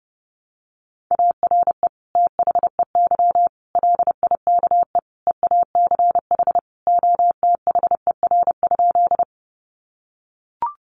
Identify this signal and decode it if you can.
Morse code, heard as a keyed tone, about 30 wpm, 700 Hz. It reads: ARE THEY LIKE EACH OTHER?
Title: Sentences from Top 200 Words 30wpm